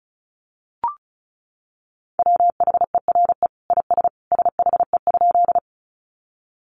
Morse code, heard as a keyed tone, about 34 wpm, 700 Hz. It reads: WHERE IS SHE?
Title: Sentences from Top 200 Words 35wpm